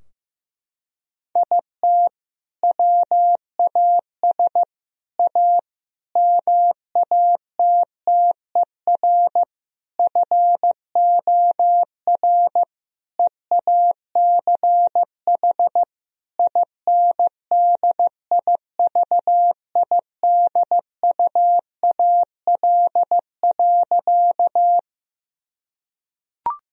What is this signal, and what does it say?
Morse code, heard as a keyed tone, about 15 wpm, 700 Hz. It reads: IT WAS A MATTER FOR EACH INDIVIDUAL.
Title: Sentences from Top 1000 Words - Review Part 4 - 15wpm